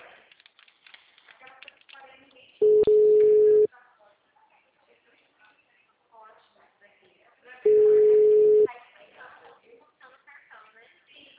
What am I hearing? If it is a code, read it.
Ringback tone (Europe, 425 Hz), with speech.
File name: out- -2016- - - .8984